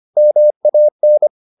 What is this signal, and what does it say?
Morse code, heard as a keyed tone, about 25 wpm, 600 Hz. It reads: MAN